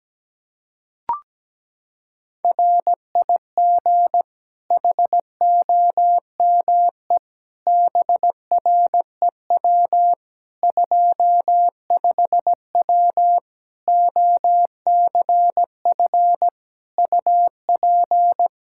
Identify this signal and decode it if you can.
Morse code, heard as a keyed tone, about 17 wpm, 700 Hz. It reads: RIG HOME BREW 25W OCF UP
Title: Practice QSO - Exchange 2 - Info - 17wpm